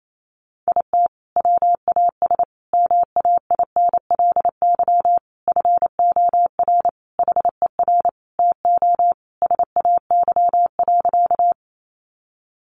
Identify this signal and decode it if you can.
Morse code, heard as a keyed tone, about 28 wpm, 700 Hz. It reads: IT WAS MAINLY FOR HER TO SAY.
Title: Sentences from Top 1400 Words 28wpm